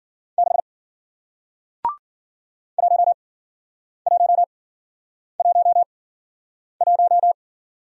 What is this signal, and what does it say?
Morse code, heard as a keyed tone, about 53 wpm, spaced as for 9 wpm, 700 Hz. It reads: 6 1111